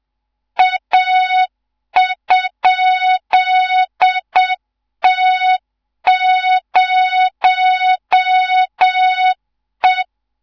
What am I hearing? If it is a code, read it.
Morse code, heard as a keyed tone, about 7 wpm, 3700 Hz. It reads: A?T0E
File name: Morze-HA2MN / hang / 14 280 HA2MN